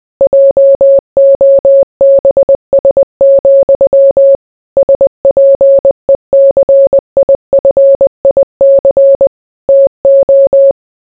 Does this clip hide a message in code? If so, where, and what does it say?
Yes: Morse code from 0.2 to 10.7 s, heard as a keyed tone, about 20 wpm, 550 Hz. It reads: JOBS, SPECIFIC TO